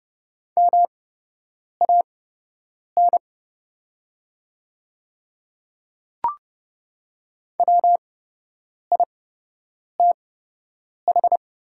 Morse code, heard as a keyed tone, about 29 wpm, spaced as for 8 wpm, 700 Hz. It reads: MAN WITH
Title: 8x Character Spacing - Top 200 Words - 30wpm